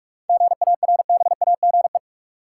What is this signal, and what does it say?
Morse code, heard as a keyed tone, about 46 wpm, 700 Hz. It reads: GARBAGE
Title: Seven Letter Words - 45wpm